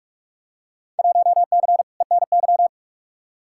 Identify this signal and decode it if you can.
Morse code, heard as a keyed tone, about 44 wpm, 700 Hz. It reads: 1C ENY
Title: Field Day Exchanges Encore - 45wpm